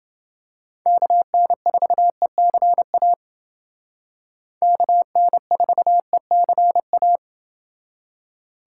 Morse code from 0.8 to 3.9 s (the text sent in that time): KN4ECA